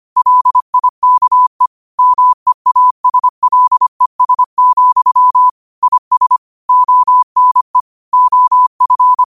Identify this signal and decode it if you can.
Morse code, heard as a keyed tone, about 25 wpm, 1000 Hz. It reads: LIKE MEASLES, IS ONE OF
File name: qotdfiles / 25wpm